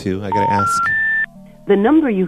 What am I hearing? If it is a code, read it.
Special information tone, with speech.